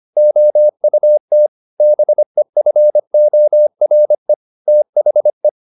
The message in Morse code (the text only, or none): OUT BEFORE THE